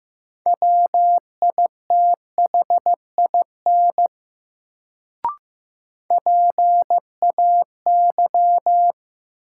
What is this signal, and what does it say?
Morse code, heard as a keyed tone, about 16 wpm, 700 Hz. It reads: WITHIN PAY
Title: Sentences from Top 700 Words - New Word Drill - 15wpm